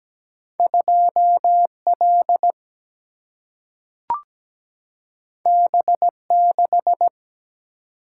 Morse code, heard as a keyed tone, about 17 wpm, 700 Hz. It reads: 2 L B 6
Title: Single-Letter Single-Number 17wpm